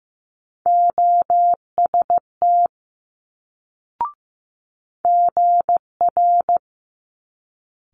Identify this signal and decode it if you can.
Morse code, heard as a keyed tone, about 15 wpm, 700 Hz. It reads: OST GR
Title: Multi-letter Phonemes Quick Fire 15wpm